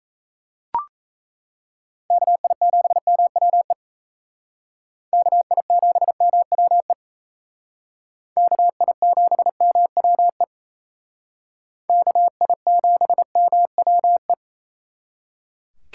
Morse code, heard as a keyed tone, about 36 wpm, 700 Hz. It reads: KI7MWE KI7MWE KI7MWE KI7MWE